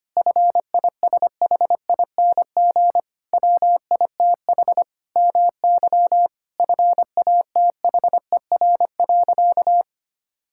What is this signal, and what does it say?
Morse code, heard as a keyed tone, about 25 wpm, 700 Hz. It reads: FISHING WITH MY FATHER.
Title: Sentences from Top 1000 Words - Review Part 5 - 25wpm